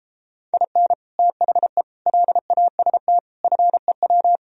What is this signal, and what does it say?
Morse code, heard as a keyed tone, about 33 wpm, 700 Hz. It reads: IN THE LAST FEW